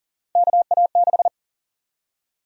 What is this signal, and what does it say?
Morse code, heard as a keyed tone, about 37 wpm, 700 Hz. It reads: KA6
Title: USA Call Sign Prefixes - 40wpm